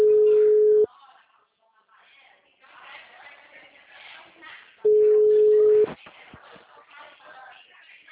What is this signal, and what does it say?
Ringback tone (Europe, 425 Hz), with speech.
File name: out- -2011- - - .55644